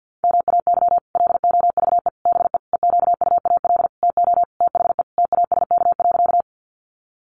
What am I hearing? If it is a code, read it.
Morse code, heard as a keyed tone, about 48 wpm, 700 Hz. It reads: MAY LOVE BE EQUAL TO THE TASK.
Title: Sentences from Top 1400 Words 50wpm